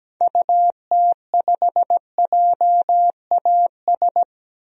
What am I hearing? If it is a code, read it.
Morse code, heard as a keyed tone, about 17 wpm, 700 Hz. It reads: UT5JAS
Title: DXCC-100 Call Signs - 6 Characters - 17wpm